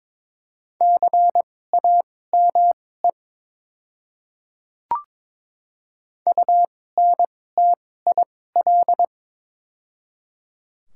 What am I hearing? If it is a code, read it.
Morse code, heard as a keyed tone, about 22 wpm, spaced as for 16 wpm, 700 Hz. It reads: CAME UNTIL
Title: 2x Character Spacing - Top 300 Words - 22wpm